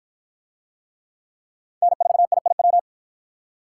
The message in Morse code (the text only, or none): N4IIW